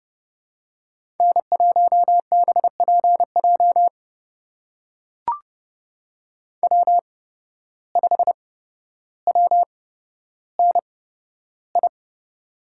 Morse code, heard as a keyed tone, about 31 wpm, spaced as for 13 wpm, 700 Hz. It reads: N1BPJ W5WNI
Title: USA Call Signs - 5 Characters - 8x Character Spacing - 30wpm